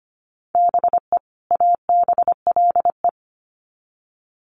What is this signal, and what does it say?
Morse code, heard as a keyed tone, about 25 wpm, 700 Hz. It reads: BE ABLE
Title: Sets of 2 Words 25wpm